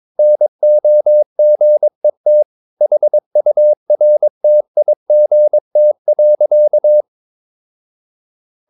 Morse code, heard as a keyed tone, about 23 wpm, 600 Hz. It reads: NOGET HURTIGT.
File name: Ferie QRS 22wpm